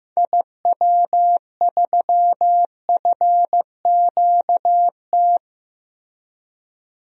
Morse code, heard as a keyed tone, about 15 wpm, 700 Hz. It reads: IW3FQT
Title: DXCC-100 Call Signs - 6 Characters - Repeated 3x - 15wpm